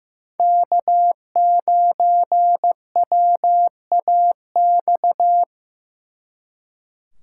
Morse code, heard as a keyed tone, about 18 wpm, 700 Hz. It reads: K9WAX
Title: Speed Racing - USA 1x3 Call Signs 15wpm